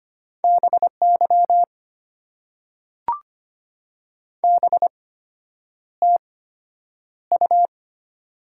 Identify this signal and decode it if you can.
Morse code, heard as a keyed tone, about 25 wpm, spaced as for 9 wpm, 700 Hz. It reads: BY BTU
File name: Top 90 QSO Elements - 8x Character Spacing - 25wpm